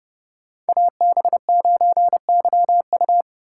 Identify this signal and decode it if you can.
Morse code, heard as a keyed tone, about 36 wpm, 700 Hz. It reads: AB9YU
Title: Speed Racing - USA 2x2 Call Signs 30wpm